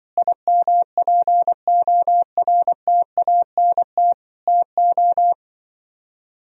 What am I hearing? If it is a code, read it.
Morse code, heard as a keyed tone, about 24 wpm, 700 Hz. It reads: IMPORTANT TO
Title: Speed-Racing - Sets of 4 Words 20wpm